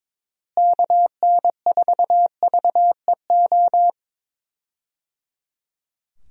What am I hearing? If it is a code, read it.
Morse code, heard as a keyed tone, about 22 wpm, 700 Hz. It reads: KN4VEO
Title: USA Call Signs - 6 Characters - Repeated 3x - 22wpm